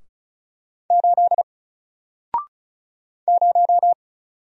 Morse code, heard as a keyed tone, about 36 wpm, 700 Hz. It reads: 8 0
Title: Single Letter-Number Alt 35wpm